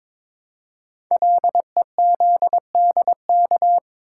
Morse code, heard as a keyed tone, about 22 wpm, 700 Hz. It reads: LEZDK
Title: Code Groups - 5 Letters - 22wpm